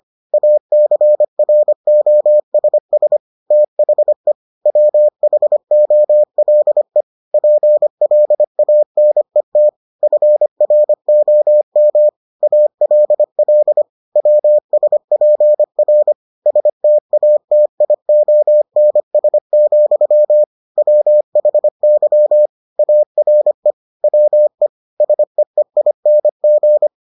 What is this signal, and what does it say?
Morse code, heard as a keyed tone, about 25 wpm, 600 Hz. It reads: ACROSS THE WHOLE PLANET FROM ALL WSPR STATIONS, WHY ARE WE SEEING